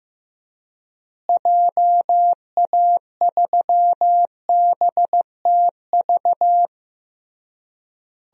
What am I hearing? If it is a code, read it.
Morse code, heard as a keyed tone, about 15 wpm, 700 Hz. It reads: JA3BTV
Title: DXCC-100 Call Signs - 6 Characters - Repeated 3x - 15wpm